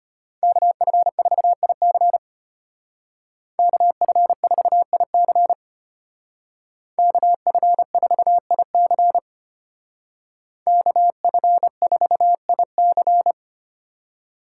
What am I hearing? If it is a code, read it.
Morse code, heard as a keyed tone, about 33 wpm, 700 Hz. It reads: KF4IC KF4IC KF4IC KF4IC